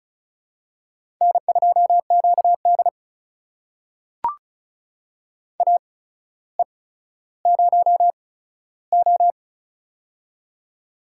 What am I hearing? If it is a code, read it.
Morse code, heard as a keyed tone, about 36 wpm, spaced as for 15 wpm, 700 Hz. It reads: N2QD AE0O
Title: USA Call Signs - 4 Characters - 8x Character Spacing - 35wpm